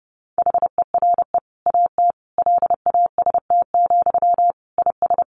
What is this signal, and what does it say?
Morse code, heard as a keyed tone, about 30 wpm, 700 Hz. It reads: HERE AT LAST, IS
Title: Sentences from Top 1100 Words 30wpm